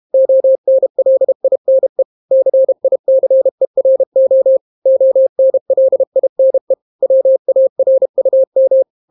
Morse code, heard as a keyed tone, about 32 wpm, 530 Hz. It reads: ONLINE CICERO ONLINE WARUM